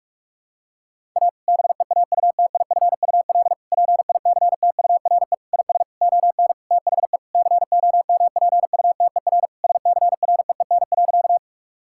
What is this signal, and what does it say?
Morse code, heard as a keyed tone, about 46 wpm, 700 Hz. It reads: A BEAUTIFUL PICTURE IS ON THE COMPUTER SCREEN.
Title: Sentences from Top 1000 Words - Review Part 6 - 45wpm